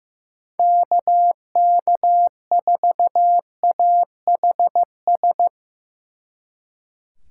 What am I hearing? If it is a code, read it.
Morse code, heard as a keyed tone, about 17 wpm, 700 Hz. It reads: KK4AHS
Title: Speed Racing - USA Call Signs 15wpm